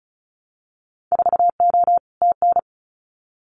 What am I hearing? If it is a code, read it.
Morse code, heard as a keyed tone, about 36 wpm, 700 Hz. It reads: 4O TN
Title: Winter Field Day Exchanges Encore 35wpm